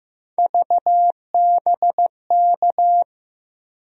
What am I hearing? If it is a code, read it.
Morse code, heard as a keyed tone, about 16 wpm, 700 Hz. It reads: VBK
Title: USA Call Sign Postfixes - 3 Letters 15wpm